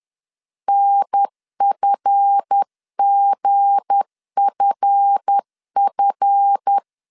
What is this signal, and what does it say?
Morse code, heard as a keyed tone, about 10 wpm, 800 Hz. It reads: NFGFF
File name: LEZIONI / 7 GF9Z / 7 GF 10wpm 2514